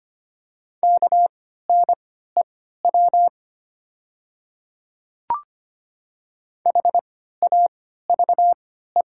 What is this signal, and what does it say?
Morse code, heard as a keyed tone, about 25 wpm, spaced as for 14 wpm, 690 Hz. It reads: KNEW HAVE
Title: 3x Character Spacing - Top 500 Words - 25wpm